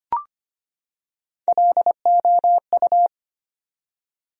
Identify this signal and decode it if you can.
Morse code, heard as a keyed tone, about 25 wpm, 700 Hz. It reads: LOU
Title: USA Common Names - Max 4 Letters - 25wpm